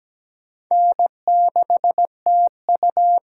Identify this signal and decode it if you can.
Morse code, heard as a keyed tone, about 17 wpm, 700 Hz. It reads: N6TU
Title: USA Call Signs - 4 Characters - 4x Character Spacing - 17wpm